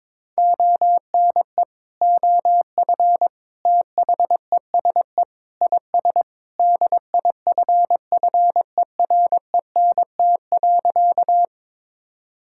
Morse code, heard as a keyed tone, about 21 wpm, 700 Hz. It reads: ONE OF THESE IS DIFFERENT.
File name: practice / 22wpm / Sentences with 2-Word Phrases - 22wpm